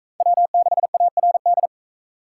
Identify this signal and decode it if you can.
Morse code, heard as a keyed tone, about 42 wpm, 700 Hz. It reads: W6ARD